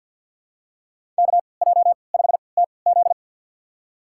Morse code, heard as a keyed tone, about 49 wpm, spaced as for 28 wpm, 700 Hz. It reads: KJ5TZ
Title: USA Call Signs - 5 Characters - 3x Character Spacing - 50wpm